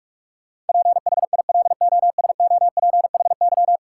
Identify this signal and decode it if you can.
Morse code, heard as a keyed tone, about 46 wpm, 700 Hz. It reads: PHILOSOPHY
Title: Sentences from Top 900 Words - New Word Drill - 45wpm